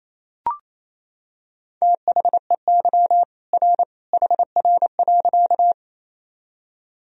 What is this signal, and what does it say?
Morse code, heard as a keyed tone, about 28 wpm, 700 Hz. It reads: THEY R HR.
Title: Sentences from Top 200 Words and Abbreviations 28wpm